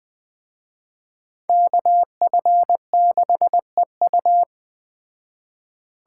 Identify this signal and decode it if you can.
Morse code, heard as a keyed tone, about 20 wpm, 700 Hz. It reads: KF6EU
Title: Speed Racing - USA 2x2 Call Signs 17wpm